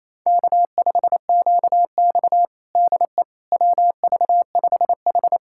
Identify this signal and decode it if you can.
Morse code, heard as a keyed tone, about 28 wpm, 700 Hz. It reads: K5QX DE WV5H
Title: USA Callsign DE Callsign - 4 Characters - 28wpm